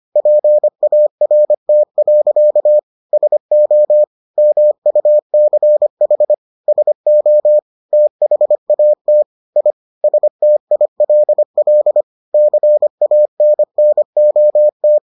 Morse code, heard as a keyed tone, about 25 wpm, 600 Hz. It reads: PART. SO MUCH SO THAT I STILL CANNOT